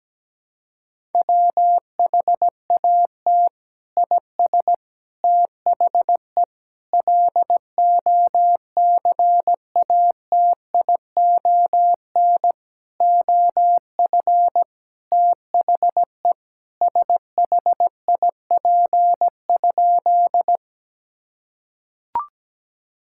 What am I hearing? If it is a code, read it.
Morse code, heard as a keyed tone, about 17 wpm, 700 Hz. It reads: WHAT IS THE LOCATION OF THE SHIP?
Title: Sentences from Top 1000 Words - Review Part 6 - 17wpm